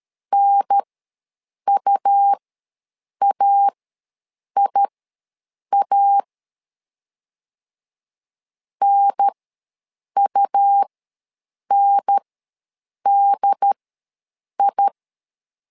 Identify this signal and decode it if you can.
Morse code, heard as a keyed tone, about 13 wpm, spaced as for 7 wpm, 800 Hz. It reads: NUAIA NUNDI
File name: LEZIONI / 2 UHDN / 2 UHDN ws 12wpm 1708